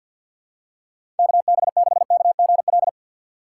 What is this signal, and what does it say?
Morse code, heard as a keyed tone, about 54 wpm, 700 Hz. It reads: K B 6 K G L
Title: USA Call Signs - 6 Characters - 8x Character Spacing - 50wpm